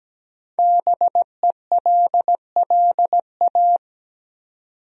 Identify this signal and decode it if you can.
Morse code, heard as a keyed tone, about 17 wpm, 700 Hz. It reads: BELLA
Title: USA Common Names - Max 5 Letters - 17wpm